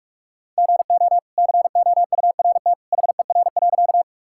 Morse code, heard as a keyed tone, about 46 wpm, 700 Hz. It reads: GO COURT HER.
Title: Sentences from Top 1600 Words 45wpm